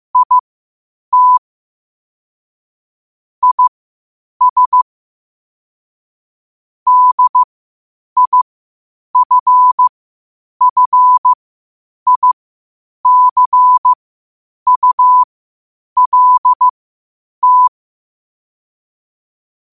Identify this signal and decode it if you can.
Morse code, heard as a keyed tone, about 15 wpm, spaced as for 8 wpm, 1000 Hz. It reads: IT IS DIFFICULT
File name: 8wpm